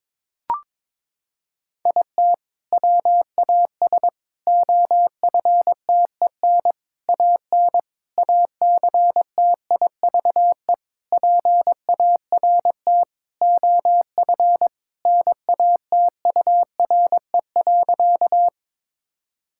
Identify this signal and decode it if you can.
Morse code, heard as a keyed tone, about 22 wpm, 700 Hz. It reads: IT WAS OFTEN AN ACTIVE PART OF NATURE.
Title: Sentences from Top 1000 Words - Review Part 6 - 22wpm